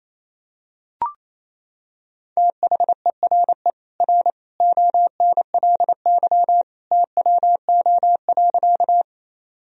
Morse code, heard as a keyed tone, about 28 wpm, 700 Hz. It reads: THERE R ONLY TWO.
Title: Sentences from Top 200 Words and Abbreviations 28wpm